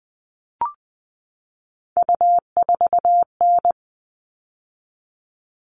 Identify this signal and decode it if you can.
Morse code, heard as a keyed tone, about 20 wpm, 700 Hz. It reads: U4N
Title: All Cut Numbers - 3-Digit 20wpm